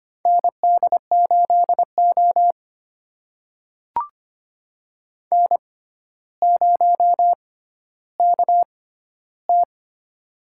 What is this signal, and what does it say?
Morse code, heard as a keyed tone, about 25 wpm, spaced as for 13 wpm, 700 Hz. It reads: ND8O N0KT
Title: USA Call Signs - 4 Characters - 6x Character Spacing - 25wpm